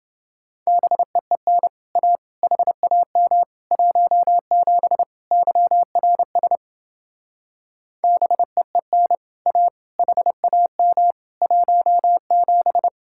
Morse code, heard as a keyed tone, about 28 wpm, 700 Hz. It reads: BEEN A HAM 17 YRS BEEN A HAM 17